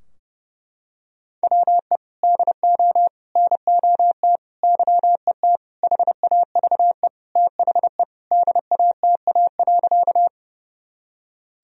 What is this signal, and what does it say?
Morse code, heard as a keyed tone, about 32 wpm, 700 Hz. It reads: WE DO NOT YET HAVE THE DATA.
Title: Sentences from Top 1000 Words - Review Part 3 - 30wpm